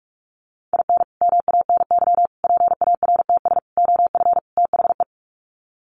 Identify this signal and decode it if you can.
Morse code, heard as a keyed tone, about 44 wpm, 700 Hz. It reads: IN MANY PARTS OF THE
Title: Sets of 5 Words Encore 45wpm